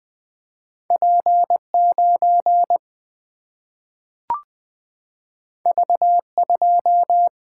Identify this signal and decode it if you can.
Morse code, heard as a keyed tone, about 20 wpm, 700 Hz. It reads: P9 V2
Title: Single-Letter Single-Number 20wpm